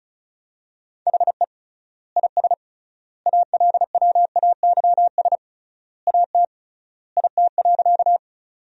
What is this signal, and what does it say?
Morse code, heard as a keyed tone, about 36 wpm, 700 Hz. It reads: HE IS ALWAYS AT IT.
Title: 3x Word-Spacing - Sentences from Top 300 Words - 35wpm